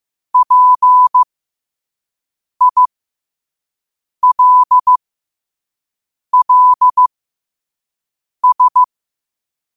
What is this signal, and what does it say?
Morse code, heard as a keyed tone, about 15 wpm, spaced as for 5 wpm, 1000 Hz. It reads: PILLS
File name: qotdfiles / 5wpm